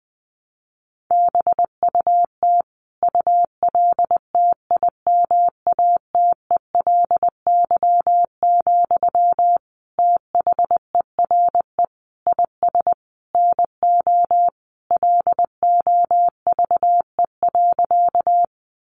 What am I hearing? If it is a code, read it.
Morse code, heard as a keyed tone, about 20 wpm, 700 Hz. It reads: BUT ULTIMATELY, THERE IS NO LOVE.
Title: Sentences from Top 1200 Words 20wpm